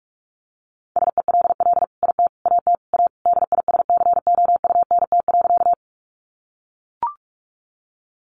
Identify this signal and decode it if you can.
Morse code, heard as a keyed tone, about 44 wpm, 700 Hz. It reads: SELL IT AT A DISCOUNT.
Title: Sentences from Top 1200 Words 45wpm